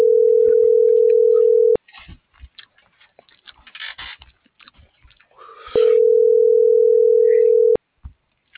Ringback tone (North America), 440 + 480 Hz.